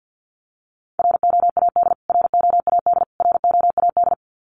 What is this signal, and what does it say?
Morse code, heard as a keyed tone, about 50 wpm, 700 Hz. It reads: ROAD ROAD ROAD
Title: Top 500 Words Repeated 50wpm